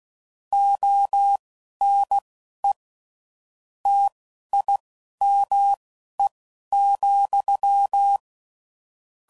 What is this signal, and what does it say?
Morse code, heard as a keyed tone, about 16 wpm, spaced as for 11 wpm, 800 Hz. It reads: ONE TIME,
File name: CW Code Groups @10WPM/Technical Subjects